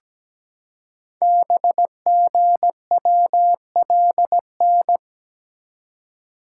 Morse code, heard as a keyed tone, about 17 wpm, 700 Hz. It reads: BGWLN
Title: Code Groups - 5 Letters - 17wpm